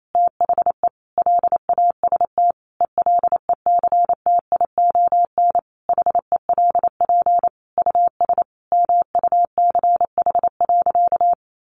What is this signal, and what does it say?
Morse code, heard as a keyed tone, about 28 wpm, 700 Hz. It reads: THE LAST ELECTION HELP US MUCH.
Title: Sentences from Top 1500 Words 28wpm